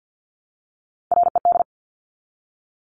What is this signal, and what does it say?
Morse code, heard as a keyed tone, about 50 wpm, 700 Hz. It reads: RED
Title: Top 500 Words Repeated 50wpm